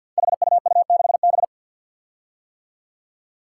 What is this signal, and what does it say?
Morse code, heard as a keyed tone, about 50 wpm, 700 Hz. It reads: HUU6B
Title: Code Groups - 5 Letter:Numbers - 50wpm